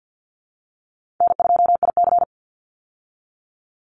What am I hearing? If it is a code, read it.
Morse code, heard as a keyed tone, about 50 wpm, 700 Hz. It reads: N2IC
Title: Call Signs Repeated 2x - 50wpm